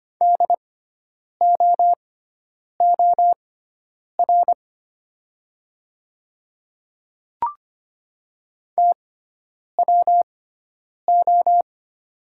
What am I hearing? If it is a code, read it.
Morse code, heard as a keyed tone, about 25 wpm, spaced as for 9 wpm, 700 Hz. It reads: DOOR TWO
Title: 6x Character Spacing - Top 300 Words - 25wpm